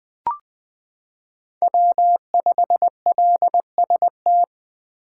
Morse code, heard as a keyed tone, about 20 wpm, 700 Hz. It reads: W5LST
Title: 2023 POTA Activator Call Signs - 20wpm